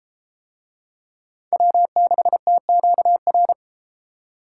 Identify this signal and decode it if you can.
Morse code, heard as a keyed tone, about 32 wpm, 700 Hz. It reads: W6TQR